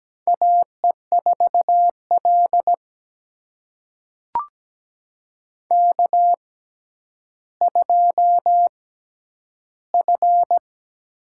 Morse code, heard as a keyed tone, about 17 wpm, spaced as for 10 wpm, 700 Hz. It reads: AE4L K2F
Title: USA Call Signs - 6x Character Spacing - 17wpm